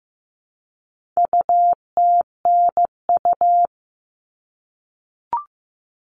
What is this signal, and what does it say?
Morse code, heard as a keyed tone, about 15 wpm, 700 Hz. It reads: UTNU
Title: All Cut Numbers - 4-Digit 15wpm